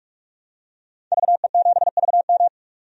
Morse code, heard as a keyed tone, about 43 wpm, 700 Hz. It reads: VE7VM
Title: DXCC-100 Call Signs - 5 Characters - 45wpm